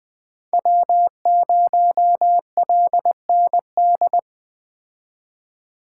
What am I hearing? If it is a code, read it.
Morse code, heard as a keyed tone, about 23 wpm, 700 Hz. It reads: W0LND